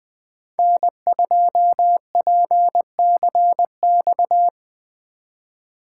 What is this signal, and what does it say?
Morse code, heard as a keyed tone, about 24 wpm, 700 Hz. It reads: N2PCX
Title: Speed Racing - USA 1x3 Call Signs 20wpm